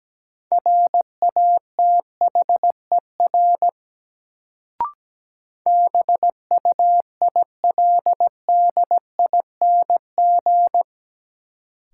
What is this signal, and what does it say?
Morse code, heard as a keyed tone, about 17 wpm, 700 Hz. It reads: RATHER BUILDING
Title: Sentences from Top 700 Words - New Word Drill - 17wpm